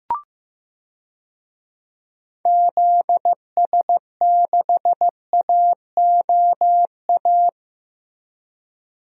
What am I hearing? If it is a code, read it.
Morse code, heard as a keyed tone, about 15 wpm, 700 Hz. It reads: ZS6AOA
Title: DXCC-100 Call Signs - 6 Characters - Repeated 3x - 15wpm